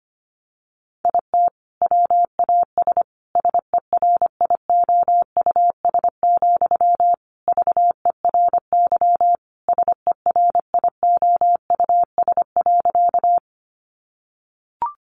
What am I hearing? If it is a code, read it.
Morse code, heard as a keyed tone, about 25 wpm, 700 Hz. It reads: IT WAS SERIOUS, VERY SERIOUS.
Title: Sentences from Top 1100 Words 25wpm